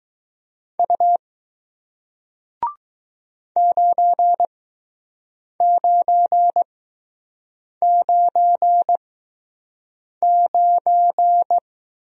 Morse code, heard as a keyed tone, about 20 wpm, spaced as for 6 wpm, 700 Hz. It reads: U 9999